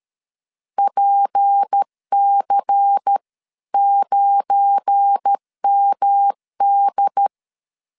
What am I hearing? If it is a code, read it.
Morse code, heard as a keyed tone, about 13 wpm, 800 Hz. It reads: PC9MD
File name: LEZIONI / 7 GF9Z / 7 GF9 12wpm 2319